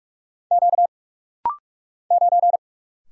Morse code, heard as a keyed tone, about 44 wpm, 700 Hz. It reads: Q 9
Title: Single Letter-Number Alt - Rapid-Fire 45wpm